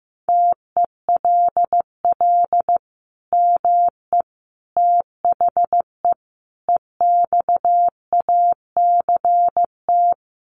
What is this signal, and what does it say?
Morse code, heard as a keyed tone, about 15 wpm, 700 Hz. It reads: TELL ME THE EXACT
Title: Sentences from Top 1300 Words 15wpm